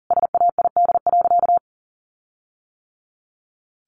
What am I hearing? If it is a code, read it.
Morse code, heard as a keyed tone, about 40 wpm, 700 Hz. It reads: SAID.
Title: Sentences from Top 1300 Words 40wpm